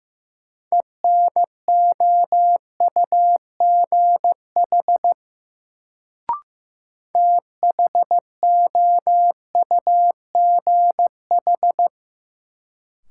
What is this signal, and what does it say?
Morse code, heard as a keyed tone, about 16 wpm, 700 Hz. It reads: ENOUGH THOUGH
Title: Sentences from Top 400 Words - New Word Drill - 15wpm